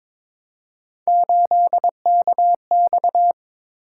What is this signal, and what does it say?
Morse code, heard as a keyed tone, about 22 wpm, 700 Hz. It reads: 8KX
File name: practice / 22wpm / Three Letter-Number Combinations - 22wpm